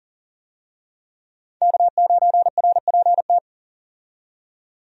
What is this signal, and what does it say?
Morse code, heard as a keyed tone, about 40 wpm, 700 Hz. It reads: K9RPT
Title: USA Call Signs - 5 Characters - 40wpm